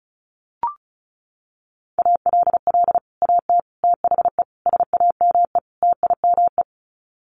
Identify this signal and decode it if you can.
Morse code, heard as a keyed tone, about 34 wpm, 700 Hz. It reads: ALL AT THE SAME TIME
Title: Sets of 5 Words Encore2 35wpm